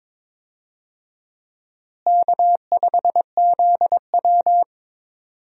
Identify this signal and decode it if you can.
Morse code, heard as a keyed tone, about 22 wpm, 700 Hz. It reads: K5ZW
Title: USA Call Signs - 4 Characters - Repeated 3x - 22wpm